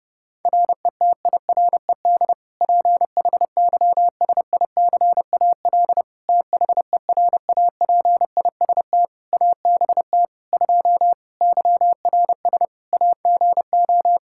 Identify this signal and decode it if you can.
Morse code, heard as a keyed tone, about 30 wpm, 700 Hz. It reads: RETIRED PHYSICAL THERAPIST ABT 2 YRS AGO
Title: K1USN SST QSO Practice 30wpm